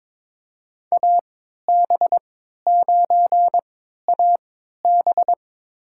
Morse code, heard as a keyed tone, about 22 wpm, spaced as for 13 wpm, 700 Hz. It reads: AB9AB
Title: USA Call Signs - 5 Characters - 3x Character Spacing - 22wpm